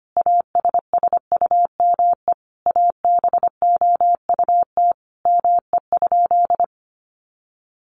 Morse code, heard as a keyed tone, about 25 wpm, 700 Hz. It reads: ASSUME ABOUT ME?
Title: Sentences from Top 1500 Words 25wpm